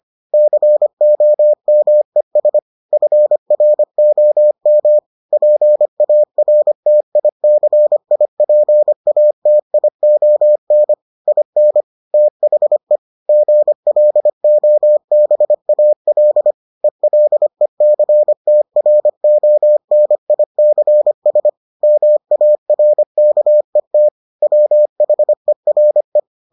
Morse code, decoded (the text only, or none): COMES FROM PARTICIPATION IN THE GLOBAL ELECTRONICS MARKET WHERE